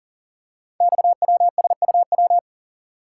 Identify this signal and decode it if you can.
Morse code, heard as a keyed tone, about 38 wpm, 690 Hz. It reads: XWSUW
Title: Code Groups - 5 Letters - 40wpm